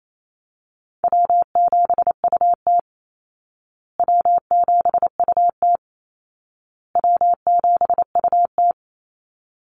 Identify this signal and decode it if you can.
Morse code, heard as a keyed tone, about 28 wpm, 700 Hz. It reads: W7UT W7UT W7UT
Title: Call Signs Repeated 2x - 28wpm